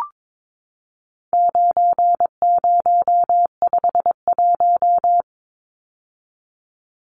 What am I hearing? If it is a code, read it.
Morse code, heard as a keyed tone, about 22 wpm, 700 Hz. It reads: 9051
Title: Numbers - 4-Digit 22wpm